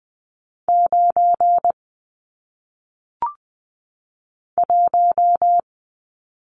Morse code, heard as a keyed tone, about 20 wpm, 700 Hz. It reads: 9 1
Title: Numbers - 1-Digit 20wpm